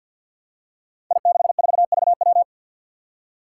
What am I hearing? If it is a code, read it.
Morse code, heard as a keyed tone, about 52 wpm, 700 Hz. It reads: I64VW
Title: Five Challenging Letter-Number Combinations 50wpm